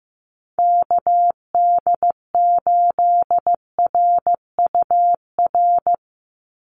Morse code, heard as a keyed tone, about 15 wpm, 700 Hz. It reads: KD8RUR
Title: USA Call Signs 15wpm